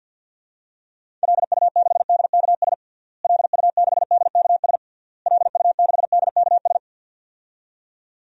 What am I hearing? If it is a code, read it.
Morse code, heard as a keyed tone, about 49 wpm, 700 Hz. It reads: LU6DKS LU6DKS LU6DKS
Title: DXCC-100 Call Signs - 6 Characters - Repeated 3x - 50wpm